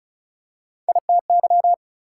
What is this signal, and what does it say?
Morse code, heard as a keyed tone, about 34 wpm, 700 Hz. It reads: ITY
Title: Prefixes and Suffixes 35wpm